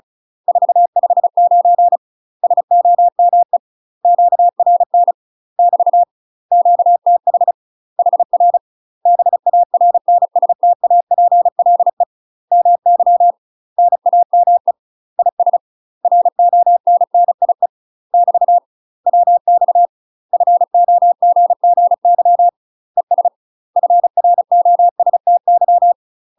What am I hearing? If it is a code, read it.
Morse code, heard as a keyed tone, about 36 wpm, 700 Hz. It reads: 459 SOME QRN <BT> QTH HR BARNSTAPLE MY NAME IS RONNIE <BT> WX FOGGY ES FROSTY
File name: UK MASTER QSOs TRAINING / 35WPM / UK MASTER QSOs 0032